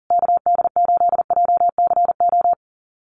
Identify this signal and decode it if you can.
Morse code, heard as a keyed tone, about 40 wpm, 700 Hz. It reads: KD8JCO
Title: USA Call Signs Repeated 3x 40wpm